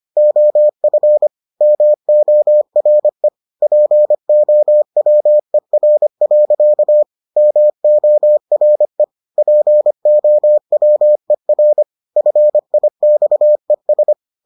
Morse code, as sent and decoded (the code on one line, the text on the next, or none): --- ..-. / -- --- .-. / .--. --- .-- . .-. .-.-.- / -- --- .-. / .--. --- .-- . .-. / ..-. .. -..- . ...
OF MORE POWER. MORE POWER FIXES